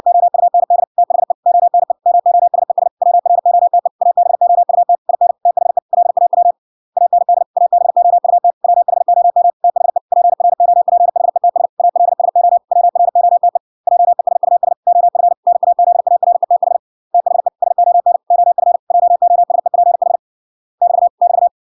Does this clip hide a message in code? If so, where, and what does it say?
Yes: Morse code from 0.1 to 21.5 s, heard as a keyed tone, about 66 wpm, 700 Hz. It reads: OUND THE ONE MOSES WROTE ABOUT IN THE LAW AND ABOUT WHOM THE PROPHETS ALSO WROTE JESUS OF NAZARETH THE SON OF JOSEPH <BT> <BT>